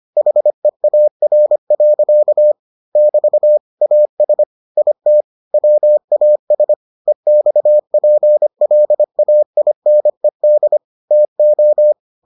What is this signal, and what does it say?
Morse code, heard as a keyed tone, about 25 wpm, 600 Hz. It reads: HEAR. <BT> AS IT WAS EXPLAINED TO